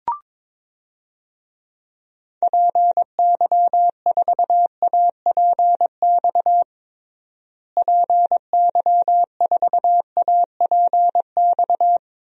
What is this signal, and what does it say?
Morse code, heard as a keyed tone, about 22 wpm, 700 Hz. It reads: PY4APX PY4APX
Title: DXCC-100 Call Signs - 6 Characters - Repeated 3x - 22wpm